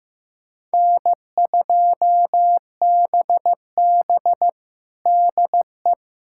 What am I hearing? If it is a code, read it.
Morse code, heard as a keyed tone, about 15 wpm, 700 Hz. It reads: N2BB DE
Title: USA Callsign DE Callsign - 4 Characters - 15wpm